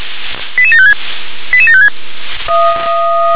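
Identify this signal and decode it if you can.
Five-tone selective call (ZVEI), heard as quick runs of five tones, tones of about 70 ms. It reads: 88756 88756
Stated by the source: ZVEI five-tone selective call received by SDR.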